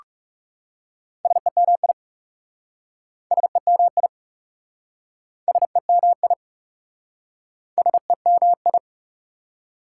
Morse code, heard as a keyed tone, about 38 wpm, 690 Hz. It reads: SEMI SEMI SEMI SEMI